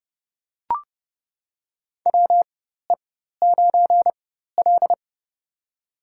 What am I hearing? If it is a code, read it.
Morse code, heard as a keyed tone, about 31 wpm, spaced as for 14 wpm, 700 Hz. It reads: WE9L
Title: USA Call Signs - 4x Character Spacing - 30wpm